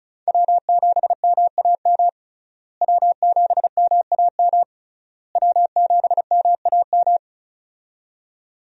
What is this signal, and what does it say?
Morse code, heard as a keyed tone, about 36 wpm, 700 Hz. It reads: W7MAM W7MAM W7MAM
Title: USA Call Signs - 5 Characters - Repeated 3x - 35wpm